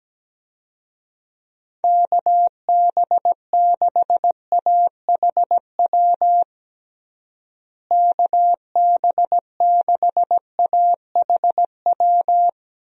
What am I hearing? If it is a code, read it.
Morse code, heard as a keyed tone, about 17 wpm, 700 Hz. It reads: KB6AHW KB6AHW
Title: USA Call Signs - 6 Characters - Repeated 3x - 17wpm